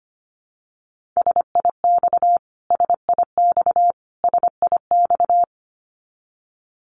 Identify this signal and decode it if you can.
Morse code, heard as a keyed tone, about 25 wpm, 700 Hz. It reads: SIX SIX SIX
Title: Top 500 Words Repeated 25wpm